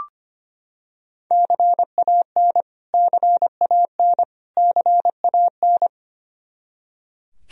Morse code, heard as a keyed tone, about 25 wpm, 700 Hz. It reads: CAN CAN CAN
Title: Top 100 Words Repeated 25wpm